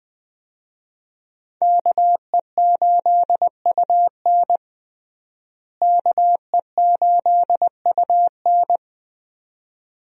Morse code, heard as a keyed tone, about 21 wpm, 700 Hz. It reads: KE8UN KE8UN
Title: USA Call Signs - 5 Characters - Repeated 2x - 20wpm